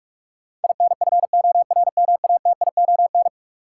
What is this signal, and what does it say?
Morse code, heard as a keyed tone, about 45 wpm, 700 Hz. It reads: INFORMATION
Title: Sentences from Top 600 Words - New Word Drill - 45wpm